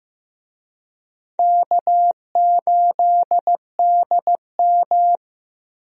Morse code, heard as a keyed tone, about 15 wpm, 700 Hz. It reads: K8DM
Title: USA Call Signs - 4 Characters - Repeated 2x - 15wpm